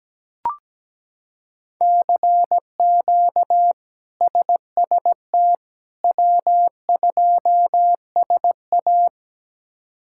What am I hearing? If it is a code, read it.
Morse code, heard as a keyed tone, about 17 wpm, 700 Hz. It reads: CQ SST W2SA
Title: K1USN SST QSO Practice - 17wpm